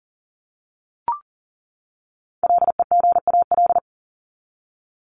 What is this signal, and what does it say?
Morse code, heard as a keyed tone, about 40 wpm, 700 Hz. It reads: LEGAL
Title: Top 1001-1250 Words 40wpm